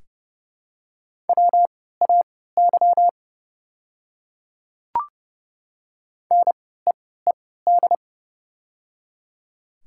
Morse code, heard as a keyed tone, about 30 wpm, spaced as for 17 wpm, 700 Hz. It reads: WAY NEED
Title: 3x Character Spacing - Top 200 Words - 30wpm